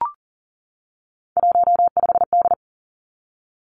Morse code, heard as a keyed tone, about 40 wpm, 700 Hz. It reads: 15D
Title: Three Challenging Letter-Number Combinations 40wpm